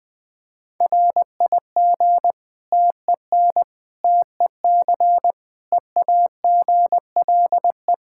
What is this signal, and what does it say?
Morse code, heard as a keyed tone, about 20 wpm, 700 Hz. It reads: RIG TEN TEC EAGLE
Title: K1USN SST QSO Practice 20wpm